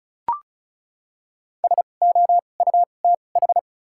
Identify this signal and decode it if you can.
Morse code, heard as a keyed tone, about 36 wpm, spaced as for 25 wpm, 700 Hz. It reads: SOUTH
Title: 2x Character Spacing - Top 500 Words - 35wpm